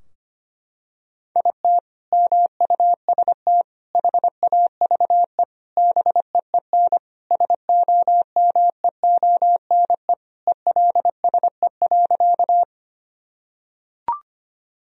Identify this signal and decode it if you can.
Morse code, heard as a keyed tone, about 25 wpm, 700 Hz. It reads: IT MUST HAVE BEEN SOMEONE ELSE.
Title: Sentences from Top 1000 Words - Review Part 4 - 25wpm